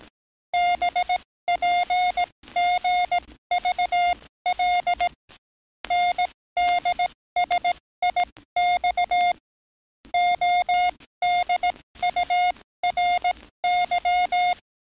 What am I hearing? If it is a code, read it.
Morse code, heard as a keyed tone, about 18 wpm, spaced as for 14 wpm, 710 Hz. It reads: BPGVL NDSIX ODURY
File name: training / NAVPERS 11049to11060 / NAVPERS 11057 RA RB